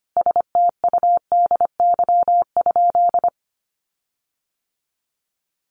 Morse code, heard as a keyed tone, about 25 wpm, 700 Hz. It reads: STUDY?
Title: Sentences from Top 1400 Words 25wpm